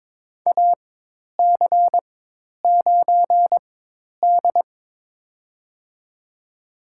Morse code, heard as a keyed tone, about 22 wpm, spaced as for 10 wpm, 700 Hz. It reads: AC9D
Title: USA Call Signs - 4x Character Spacing - 22wpm